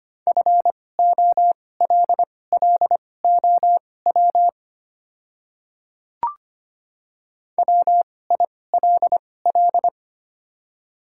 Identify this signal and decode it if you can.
Morse code, heard as a keyed tone, about 25 wpm, spaced as for 18 wpm, 700 Hz. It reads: FOLLOW WILL
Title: 2x Character Spacing - Top 300 Words - 25wpm